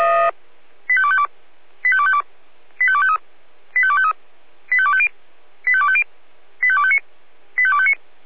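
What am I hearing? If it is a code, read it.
Five-tone selective call (ZVEI), heard as quick runs of five tones, tones of about 70 ms. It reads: 85252 85252 85253 85253 85250 85250 85259 85259
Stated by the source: ZVEI five-tone selective call received by SDR.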